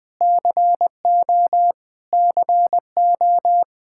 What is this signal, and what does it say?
Morse code, heard as a keyed tone, about 20 wpm, 700 Hz. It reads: CO CO
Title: Practice QSO - Exchange 1 - Beginning - 20wpm